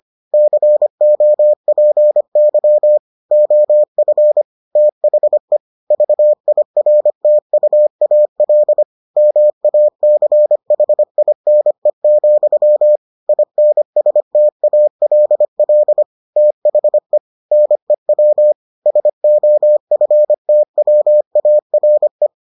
Morse code, heard as a keyed tone, about 25 wpm, 600 Hz. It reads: COPY OF THE VIRTUAL MACHINE, INSTALL THE NEW SOFTWARE